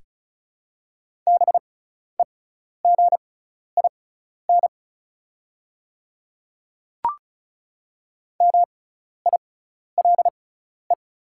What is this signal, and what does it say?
Morse code, heard as a keyed tone, about 36 wpm, spaced as for 12 wpm, 690 Hz. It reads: BEGIN MILE
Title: 6x Character Spacing - Top 300 Words - 35wpm